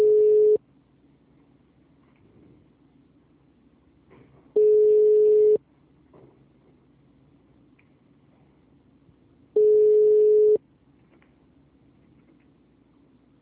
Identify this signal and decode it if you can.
Ringback tone (Europe, 425 Hz).
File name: out- -2004- - - .218